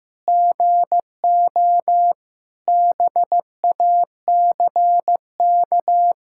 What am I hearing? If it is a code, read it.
Morse code, heard as a keyed tone, about 15 wpm, 700 Hz. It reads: GO BACK
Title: Sentences from Top 1600 Words 15wpm